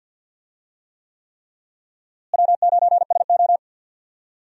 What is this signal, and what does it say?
Morse code, heard as a keyed tone, about 52 wpm, 700 Hz. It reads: W9SO
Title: USA Call Signs - 4 Characters - Repeated 3x - 50wpm